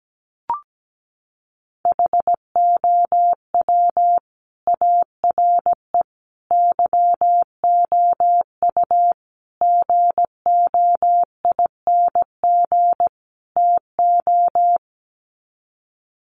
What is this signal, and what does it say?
Morse code, heard as a keyed tone, about 17 wpm, 700 Hz. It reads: HOW ARE YOU GOING TO
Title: Sets of 5 Words 17wpm